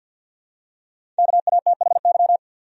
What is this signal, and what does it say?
Morse code, heard as a keyed tone, about 52 wpm, 700 Hz. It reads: KATHY
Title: USA Common Names - 50wpm